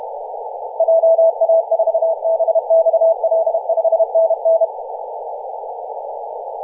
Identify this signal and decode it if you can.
Morse code, heard as a keyed tone, about 32 wpm, 660 Hz, with radio noise. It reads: JA4BXL 5NN
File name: posts / 2016-nov-23-0544z-xx9tkb-24-895mhz-cw